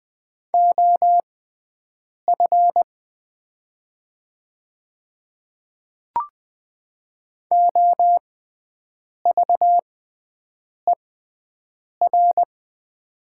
Morse code, heard as a keyed tone, about 20 wpm, spaced as for 7 wpm, 690 Hz. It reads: OF OVER
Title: 6x Character Spacing - Top 100 Words - 20wpm